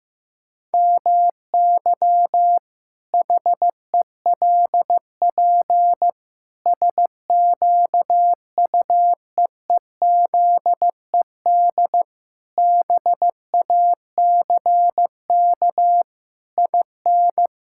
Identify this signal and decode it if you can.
Morse code, heard as a keyed tone, about 15 wpm, 700 Hz. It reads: MY HELP SQUEEZED BACK IN
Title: Pangram Challenge v2 - 15wpm